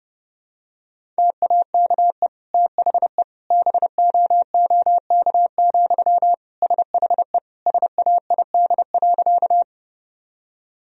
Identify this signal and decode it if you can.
Morse code, heard as a keyed tone, about 30 wpm, 700 Hz. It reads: TAKE THE BOOK, SHE SAID.
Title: Sentences from Top 1000 Words - Review Part 1 - 30wpm